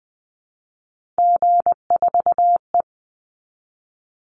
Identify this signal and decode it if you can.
Morse code, heard as a keyed tone, about 20 wpm, 700 Hz. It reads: G4E
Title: Call Signs Repeated 3x - 20wpm